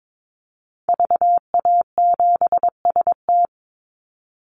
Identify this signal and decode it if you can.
Morse code, heard as a keyed tone, about 22 wpm, 700 Hz. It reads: VA7ST
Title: Call Signs Repeated 3x - 22wpm